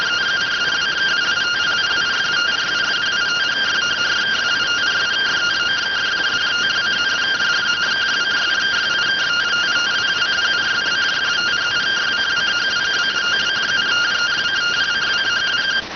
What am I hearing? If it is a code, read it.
Two-tone teleprinter signal, about 170 Hz shift.